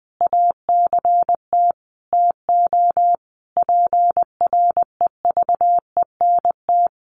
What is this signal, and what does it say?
Morse code, heard as a keyed tone, about 20 wpm, 700 Hz. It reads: ACT TO PREVENT
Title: Sentences from Top 1100 Words 20wpm